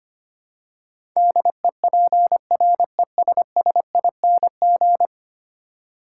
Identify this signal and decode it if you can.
Morse code, heard as a keyed tone, about 25 wpm, 700 Hz. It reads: DEPRESSING
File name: 10 Letter Words - First 3 Letters Given - 25wpm